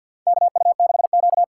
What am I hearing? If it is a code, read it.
Morse code, heard as a keyed tone, about 52 wpm, 700 Hz. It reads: KU6Q
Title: USA Call Signs - 4 Characters - 3x Character Spacing - 50wpm